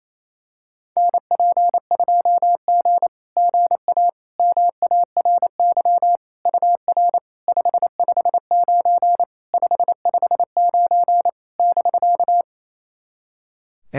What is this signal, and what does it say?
Morse code, heard as a keyed tone, about 28 wpm, 700 Hz. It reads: NP2G GA MARY UR 559 559 <BK>
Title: SOTA QSO Practice 28wpm